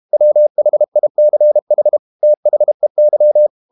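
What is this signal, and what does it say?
Morse code, heard as a keyed tone, about 32 wpm, 600 Hz. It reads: WHICH THEY
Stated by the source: Morse code practice at 32 wpm from CW med Gnister, reading Ken Follett's novel Never in English.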